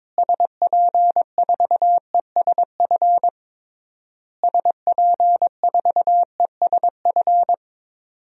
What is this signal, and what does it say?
Morse code, heard as a keyed tone, about 22 wpm, 700 Hz. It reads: SP4ESF SP4ESF
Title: DXCC-100 Call Signs - 6 Characters - Repeated 3x - 22wpm